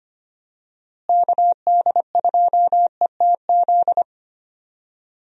Morse code, heard as a keyed tone, about 25 wpm, 700 Hz. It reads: KD2ETZ
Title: USA Call Signs - 6 Characters - Repeated 3x - 25wpm